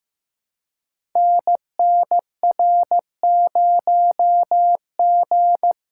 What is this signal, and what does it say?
Morse code, heard as a keyed tone, about 15 wpm, 700 Hz. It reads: NNR0G
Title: Code Groups - 5 Letter:Numbers - 15wpm